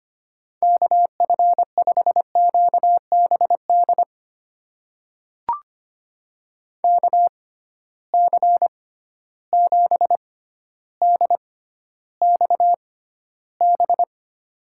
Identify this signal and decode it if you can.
Morse code, heard as a keyed tone, about 25 wpm, spaced as for 13 wpm, 700 Hz. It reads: KF5QBD KC7DXB
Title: USA Call Signs - 6 Characters - 6x Character Spacing - 25wpm